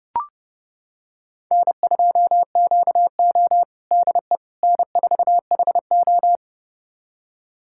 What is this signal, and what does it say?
Morse code, heard as a keyed tone, about 32 wpm, 700 Hz. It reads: N2QO DE N4HO
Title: USA Callsign DE Callsign - 4 Characters - 30wpm